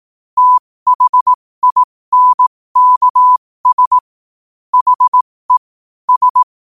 Morse code, heard as a keyed tone, about 18 wpm, spaced as for 15 wpm, 1000 Hz. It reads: THINKS HE S
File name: qotdfiles / 15wpm